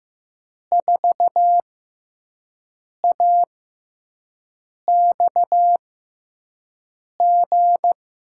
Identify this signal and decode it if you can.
Morse code, heard as a keyed tone, about 15 wpm, spaced as for 5 wpm, 700 Hz. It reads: 4AXG
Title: USA Call Signs - 5 Characters - 6x Character Spacing - 15wpm